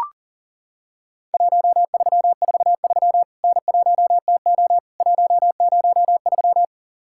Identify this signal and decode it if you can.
Morse code, heard as a keyed tone, about 40 wpm, 700 Hz. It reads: 1343 N1TO 103
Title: Worked All Europe DX Contest - QTC Exchange - 40wpm